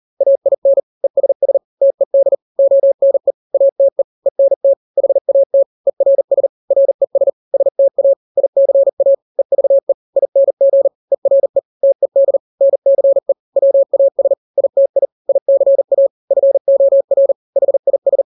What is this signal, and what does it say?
Morse code, heard as a keyed tone, about 40 wpm, 550 Hz. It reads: AIN ESS TED ONE ATE ENT HAT ERS RES STA ICA EVE ING ERE TED NCE WAS ITI ICA FOR HIS
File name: n-grams / 3-grams 40 35 WPM